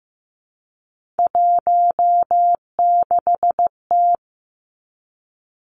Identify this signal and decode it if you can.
Morse code, heard as a keyed tone, about 15 wpm, 700 Hz. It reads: 16T
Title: Three Challenging Letter-Number Combinations 15wpm